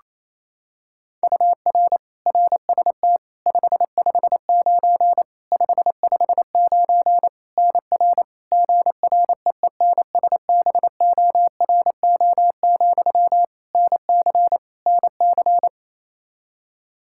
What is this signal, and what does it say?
Morse code, heard as a keyed tone, about 28 wpm, 700 Hz. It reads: UR RST 559 559 NR GREENSBORO, NC NC
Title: K1USN SST QSO Practice 28wpm